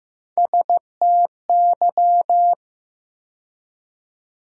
Morse code, heard as a keyed tone, about 15 wpm, 700 Hz. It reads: STY
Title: USA Call Sign Postfixes - 3 Letters 15wpm